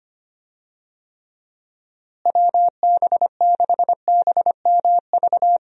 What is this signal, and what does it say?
Morse code, heard as a keyed tone, about 25 wpm, 700 Hz. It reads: WB6BMV
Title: USA Call Signs - 6 Characters - Repeated 2x - 25wpm